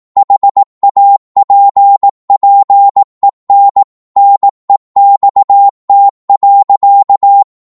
Morse code, heard as a keyed tone, about 18 wpm, 800 Hz. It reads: HAPPEN NEXT.
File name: Morse / psg-18wpm